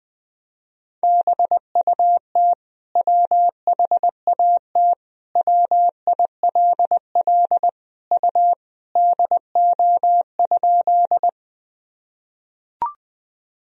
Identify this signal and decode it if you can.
Morse code, heard as a keyed tone, about 20 wpm, 700 Hz. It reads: BUT WHAT WILL U DO?
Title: Sentences from Top 100 Words and Abbreviations 20wpm